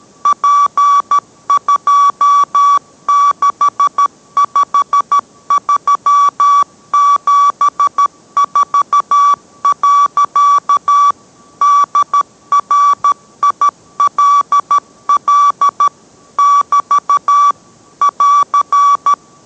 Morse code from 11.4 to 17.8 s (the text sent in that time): DRILL <BT>